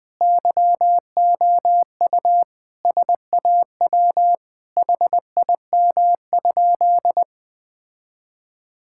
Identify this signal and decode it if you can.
Morse code, heard as a keyed tone, about 20 wpm, 700 Hz. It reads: YOU SAW HIM?
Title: Sentences with 7-Word Phrases - Spoken Beginning - 20wpm